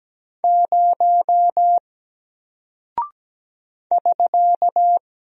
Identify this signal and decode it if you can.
Morse code, heard as a keyed tone, about 17 wpm, 700 Hz. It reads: 0 <SK>
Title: Single Character (Letter:Number:Punctuation:Prosign) 17wpm